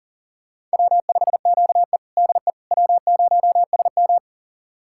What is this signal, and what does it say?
Morse code, heard as a keyed tone, about 37 wpm, 700 Hz. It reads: W5QE DE W0SM
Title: USA Callsign DE Callsign - 4 Characters - 40wpm